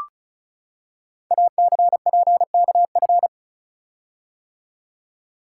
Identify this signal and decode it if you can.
Morse code, heard as a keyed tone, about 36 wpm, 700 Hz. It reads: ACPKF
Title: Code Groups - 5 Letter:Numbers - 35wpm